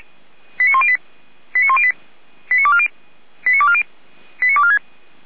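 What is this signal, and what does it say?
Five-tone selective call (ZVEI), heard as quick runs of five tones, tones of about 70 ms. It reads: 89108 89108 89240 89240 89246
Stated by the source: ZVEI five-tone call received by SDR, feuerwehr-orlamuende.org fire brigade.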